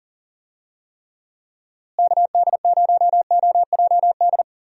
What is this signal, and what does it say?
Morse code, heard as a keyed tone, about 40 wpm, 700 Hz. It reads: KD0OJD
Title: USA Call Signs - 6 Characters - Repeated 2x - 40wpm